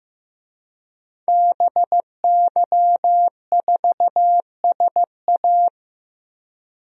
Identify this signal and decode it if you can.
Morse code, heard as a keyed tone, about 15 wpm, 700 Hz. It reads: BY4SA
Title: DXCC-100 Call Signs - 5 Characters - 15wpm